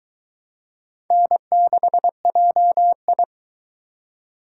Morse code, heard as a keyed tone, about 24 wpm, 700 Hz. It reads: N6JI